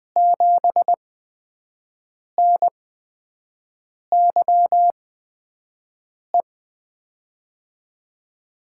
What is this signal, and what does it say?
Morse code, heard as a keyed tone, about 20 wpm, spaced as for 5 wpm, 700 Hz. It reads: 7NYE
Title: USA Call Signs - 6 Characters - 8x Character Spacing - 20wpm